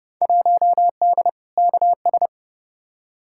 Morse code, heard as a keyed tone, about 30 wpm, 700 Hz. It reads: 1D KS
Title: Field Day Exchanges - 30wpm